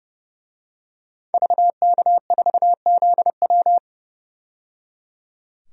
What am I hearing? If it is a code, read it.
Morse code, heard as a keyed tone, about 30 wpm, 700 Hz. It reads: VK4ZW
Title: DXCC-100 Call Signs - 5 Characters - 30wpm